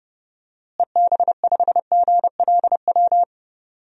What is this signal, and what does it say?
Morse code, heard as a keyed tone, about 30 wpm, 700 Hz. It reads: EB5GLW
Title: DXCC-100 Call Signs - 6 Characters - Repeated 3x - 30wpm